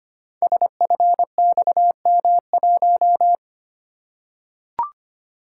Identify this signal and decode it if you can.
Morse code, heard as a keyed tone, about 25 wpm, 700 Hz. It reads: SFXM1
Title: Code Groups - 5 Letter:Numbers - 25wpm